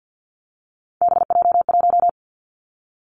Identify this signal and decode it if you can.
Morse code, heard as a keyed tone, about 50 wpm, 700 Hz. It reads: BJ1
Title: Three Challenging Letter-Number Combinations 50wpm